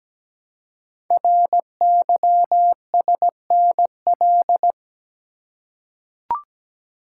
Morse code, heard as a keyed tone, about 17 wpm, 700 Hz. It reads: RYSNL
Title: Code Groups - 5 Letters - 17wpm